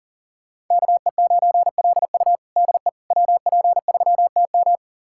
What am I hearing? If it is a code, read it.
Morse code, heard as a keyed tone, about 40 wpm, 700 Hz. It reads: KE9LU DE WP3TM
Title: USA Callsign DE Callsign - 5 Characters - 40wpm